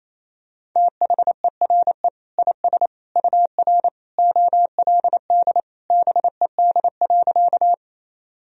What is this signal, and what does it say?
Morse code, heard as a keyed tone, about 28 wpm, 700 Hz. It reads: THERE IS UR OLD BED.
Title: Sentences from Top 500 Words and Abbreviations 28wpm